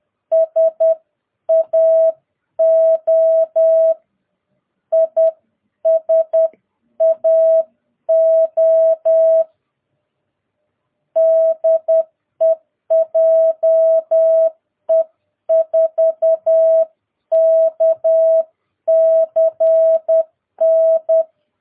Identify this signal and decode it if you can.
Morse code, heard as a keyed tone, about 10 wpm, 650 Hz. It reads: SAO ISAO DE JE4KCN
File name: jh8cbh / 2022cw